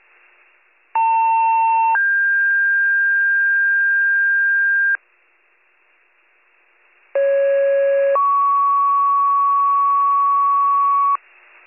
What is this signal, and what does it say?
Two-tone page, heard as a one-second tone, then a three-second tone, about 900 then 1600 Hz, about 550 then 1120 Hz. It is heard from 0.9 to 11.2 s.